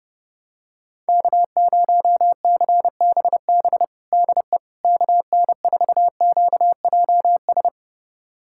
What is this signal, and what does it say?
Morse code, heard as a keyed tone, about 30 wpm, 700 Hz. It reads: K0CBB DE KN4QJS
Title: USA Callsign DE Callsign 30wpm